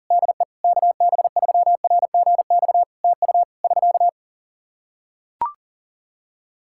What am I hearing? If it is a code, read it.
Morse code, heard as a keyed tone, about 37 wpm, 700 Hz. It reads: DE KB3RGX TU <SK>
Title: Practice QSO - Exchange 4 - Ending - 40wpm